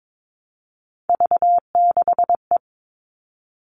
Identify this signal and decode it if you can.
Morse code, heard as a keyed tone, about 22 wpm, 700 Hz. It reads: V6E
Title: All Cut Numbers - 3-Digit 22wpm